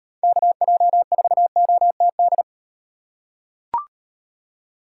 Morse code, heard as a keyed tone, about 35 wpm, 700 Hz. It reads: KJ4OTD